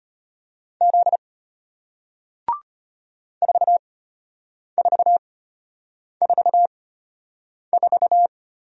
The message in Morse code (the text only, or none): Z 4444